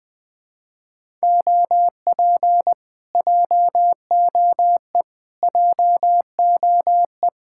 Morse code, heard as a keyed tone, about 20 wpm, 700 Hz. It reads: OP JOE JOE
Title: Speed-Racing Common QSO Phrases - 17wpm